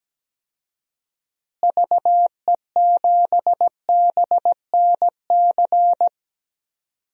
Morse code, heard as a keyed tone, about 17 wpm, 700 Hz. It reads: VE7BNC
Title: DXCC-100 Call Signs - 6 Characters - Repeated 2x - 17wpm